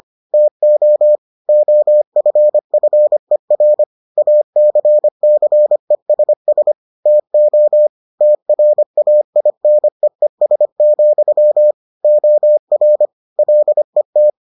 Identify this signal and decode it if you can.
Morse code, heard as a keyed tone, about 25 wpm, 600 Hz. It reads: TO OFFER ACCESS TO TRAINEES, OR LET